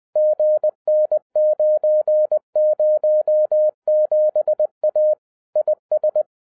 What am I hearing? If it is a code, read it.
Morse code, heard as a keyed tone, about 20 wpm, 610 Hz. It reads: GN907A IS